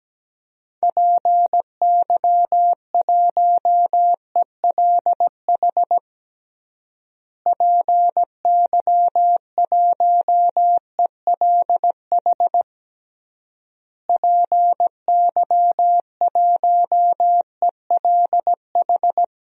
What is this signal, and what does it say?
Morse code, heard as a keyed tone, about 17 wpm, 700 Hz. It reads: PY1ELH PY1ELH PY1ELH
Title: DXCC-100 Call Signs - 6 Characters - Repeated 3x - 17wpm